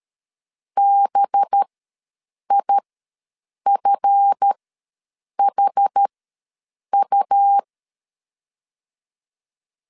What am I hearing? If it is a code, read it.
Morse code, heard as a keyed tone, about 13 wpm, spaced as for 7 wpm, 800 Hz. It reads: BIFHU
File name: LEZIONI / 7 GF9Z / 7 GF9 ws 12wpm 3344